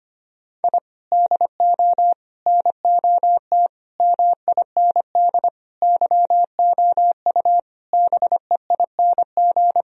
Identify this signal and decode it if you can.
Morse code, heard as a keyed tone, about 25 wpm, 700 Hz. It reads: I DO NOT MIND YOU BEING
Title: Sentences from Top 1000 Words - Review Part 3 - 25wpm